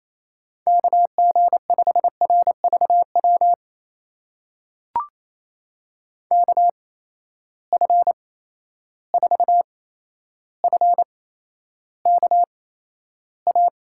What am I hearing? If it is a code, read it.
Morse code, heard as a keyed tone, about 28 wpm, spaced as for 12 wpm, 700 Hz. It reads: KG5RVW KF4FKA
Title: USA Call Signs - 6 Characters - 8x Character Spacing - 28wpm